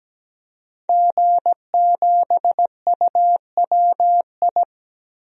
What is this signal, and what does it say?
Morse code, heard as a keyed tone, about 17 wpm, 700 Hz. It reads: G7UWI
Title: DXCC-100 Call Signs - 5 Characters - Repeated 3x - 17wpm